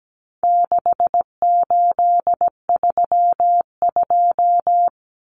Morse code, heard as a keyed tone, about 17 wpm, 700 Hz. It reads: 6832
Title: Numbers - 4-Digit 17wpm